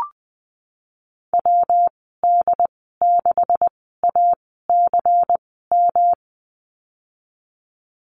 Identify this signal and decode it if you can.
Morse code, heard as a keyed tone, about 20 wpm, spaced as for 15 wpm, 700 Hz. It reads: WD6ACM
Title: 186 Lesson 30g - Letter b - Call Signs 20wpm